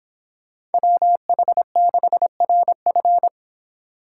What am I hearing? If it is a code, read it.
Morse code, heard as a keyed tone, about 30 wpm, 700 Hz. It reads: WH6RF